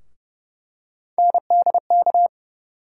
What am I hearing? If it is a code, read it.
Morse code, heard as a keyed tone, about 30 wpm, 700 Hz. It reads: NDK
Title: USA Call Sign Postfixes - 3 Letters 30wpm